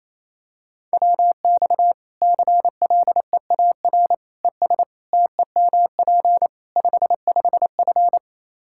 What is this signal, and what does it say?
Morse code, heard as a keyed tone, about 28 wpm, 700 Hz. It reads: WX CLEAR ES TEMP 55F
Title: Practice QSO - Exchange 2 - Info - 28wpm